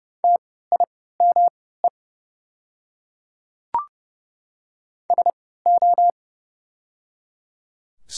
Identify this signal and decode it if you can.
Morse code, heard as a keyed tone, about 30 wpm, spaced as for 17 wpm, 700 Hz. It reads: TIME SO
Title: 3x Character Spacing - Top 100 Words - 30wpm